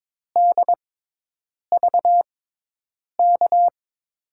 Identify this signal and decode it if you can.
Morse code, heard as a keyed tone, about 22 wpm, spaced as for 8 wpm, 700 Hz. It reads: DVK
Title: USA Call Signs - 6 Characters - 6x Character Spacing - 22wpm